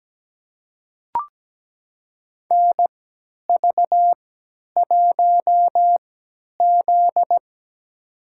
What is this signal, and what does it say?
Morse code, heard as a keyed tone, about 17 wpm, spaced as for 10 wpm, 700 Hz. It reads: NV1Z
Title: USA Call Signs - 3x Character Spacing - 17wpm